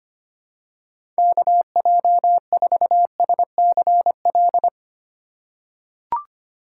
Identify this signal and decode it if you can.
Morse code, heard as a keyed tone, about 25 wpm, 700 Hz. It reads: KJ4SCL
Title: USA Call Signs - 6 Characters - 2x Character Spacing - 25wpm